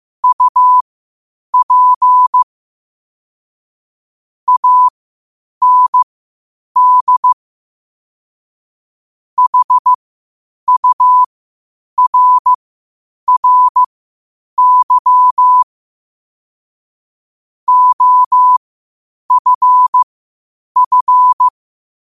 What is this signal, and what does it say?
Morse code, heard as a keyed tone, about 15 wpm, spaced as for 8 wpm, 1000 Hz. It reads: UP AND HURRY OFF
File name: qotdfiles / 8wpm